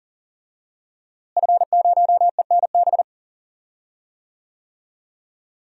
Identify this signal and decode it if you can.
Morse code, heard as a keyed tone, about 40 wpm, 700 Hz. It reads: F0ENB
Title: DXCC-100 Call Signs - 5 Characters - Repeated 3x - 40wpm